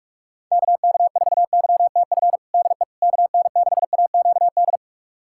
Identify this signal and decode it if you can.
Morse code, heard as a keyed tone, about 42 wpm, 690 Hz. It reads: KK4YTF DE KN6AQD
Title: USA Callsign DE Callsign - 6 Characters - 45wpm